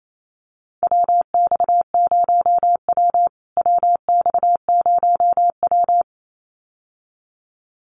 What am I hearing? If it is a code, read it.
Morse code, heard as a keyed tone, about 28 wpm, 700 Hz. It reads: WX0W WX0W
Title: USA Call Signs Repeated 2x 28wpm